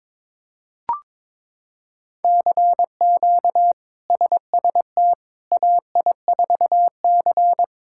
Morse code, heard as a keyed tone, about 22 wpm, 700 Hz. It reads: CQ SST AI4C
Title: K1USN SST QSO Practice - 22wpm